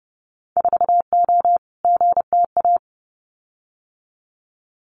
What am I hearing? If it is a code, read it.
Morse code, heard as a keyed tone, about 30 wpm, 700 Hz. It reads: 4O GTA
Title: Winter Field Day Exchanges 30wpm